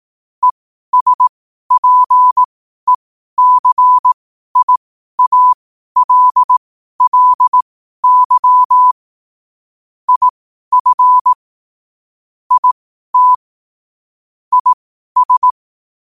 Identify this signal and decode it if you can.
Morse code, heard as a keyed tone, about 18 wpm, spaced as for 12 wpm, 1000 Hz. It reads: ESPECIALLY IF IT IS
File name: qotdfiles / 12wpm